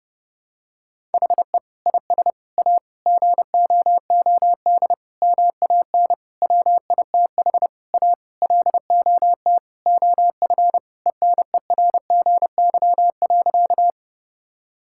Morse code, heard as a keyed tone, about 32 wpm, 700 Hz. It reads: HE IS A GOOD MAN WITH A LOT OF ENERGY.
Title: Sentences from Top 1000 Words - Review Part 6 - 30wpm